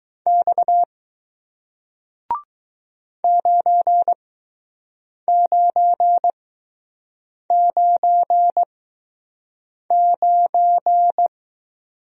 Morse code, heard as a keyed tone, about 20 wpm, spaced as for 6 wpm, 700 Hz. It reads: X 9999